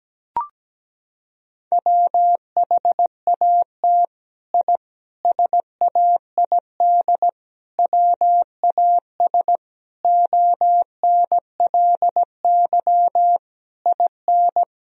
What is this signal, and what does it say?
Morse code, heard as a keyed tone, about 17 wpm, 700 Hz. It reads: WHAT I SAID WAS ONLY IN
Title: Sentences from Top 1000 Words - Review Part 4 - 17wpm